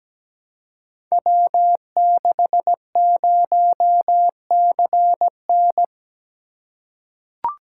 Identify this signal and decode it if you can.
Morse code, heard as a keyed tone, about 17 wpm, 700 Hz. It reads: W60CN
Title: Code Groups - 5 Letter:Numbers - 17wpm